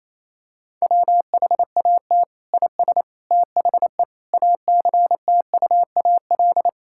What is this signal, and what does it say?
Morse code, heard as a keyed tone, about 27 wpm, 700 Hz. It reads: WHAT IS THE ACTUAL
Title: Sentences from Top 1000 Words - Review Part 5 - 28wpm